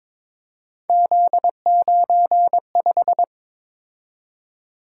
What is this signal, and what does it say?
Morse code, heard as a keyed tone, about 22 wpm, 700 Hz. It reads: Z95
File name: practice / 22wpm / Three Letter-Number Combinations - 22wpm